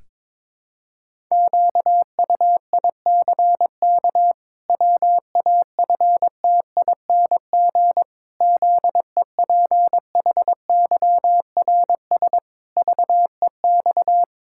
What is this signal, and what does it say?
Morse code, heard as a keyed tone, about 22 wpm, 700 Hz. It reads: QUICK WAFTING ZEPHYRS VEX
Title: Pangram Challenge v2 - 22wpm